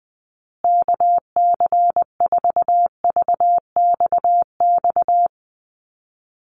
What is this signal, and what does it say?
Morse code, heard as a keyed tone, about 19 wpm, 700 Hz. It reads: KC4VXX
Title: USA Call Signs Repeated 2x 20wpm